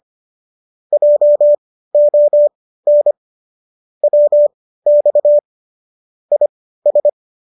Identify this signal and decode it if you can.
Morse code, heard as a keyed tone, about 25 wpm, spaced as for 15 wpm, 590 Hz. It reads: JON WX IS
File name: qso207 15